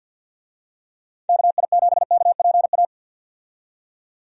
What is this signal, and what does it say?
Morse code, heard as a keyed tone, about 48 wpm, 700 Hz. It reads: KI7KPA
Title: USA Call Signs - 6 Characters - 50wpm